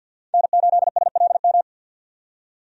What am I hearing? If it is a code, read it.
Morse code, heard as a keyed tone, about 51 wpm, 700 Hz. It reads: N8SLM
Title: USA Call Signs - 5 Characters - 50wpm